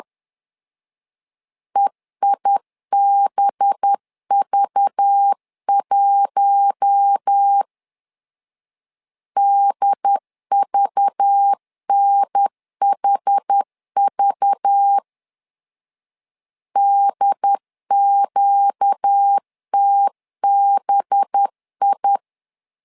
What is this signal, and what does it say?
Morse code, heard as a keyed tone, about 10 wpm, 800 Hz. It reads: EIBV1 DVNHV DQTBI
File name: LEZIONI / 5 LBVC / 5 LBV 10wpm 2248